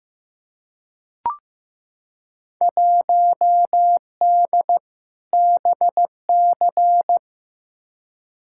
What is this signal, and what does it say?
Morse code, heard as a keyed tone, about 15 wpm, 700 Hz. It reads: 1D BC
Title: Field Day Exchanges - 15wpm